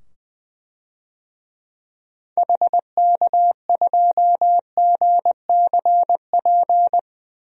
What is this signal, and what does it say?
Morse code, heard as a keyed tone, about 20 wpm, 700 Hz. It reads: HK2GCP
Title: DXCC-100 Call Signs - 6 Characters - 20wpm